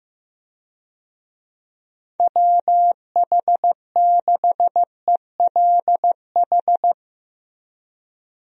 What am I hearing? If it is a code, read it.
Morse code, heard as a keyed tone, about 15 wpm, 700 Hz. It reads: WH6ELH
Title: USA Call Signs - 6 Characters - Repeated 2x - 15wpm